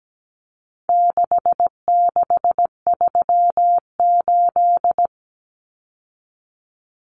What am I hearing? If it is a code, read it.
Morse code, heard as a keyed tone, about 17 wpm, 700 Hz. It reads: 6638
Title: Numbers - 4-Digit 17wpm